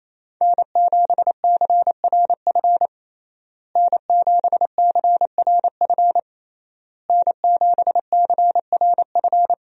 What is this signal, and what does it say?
Morse code, heard as a keyed tone, about 28 wpm, 700 Hz. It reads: N7CRF N7CRF N7CRF
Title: USA Call Signs - 5 Characters - Repeated 3x - 28wpm